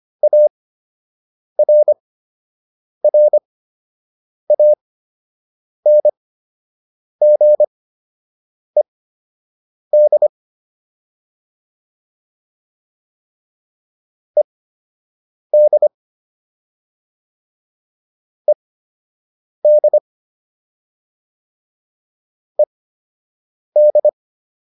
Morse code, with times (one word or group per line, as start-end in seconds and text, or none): 0.2-10.3 ARRANGED
14.4-15.9 ED
18.5-20.0 ED
22.6-24.1 ED